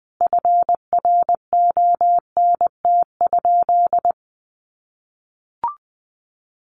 Morse code, heard as a keyed tone, about 20 wpm, 700 Hz. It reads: FRONT?
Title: Sentences from Top 1400 Words 20wpm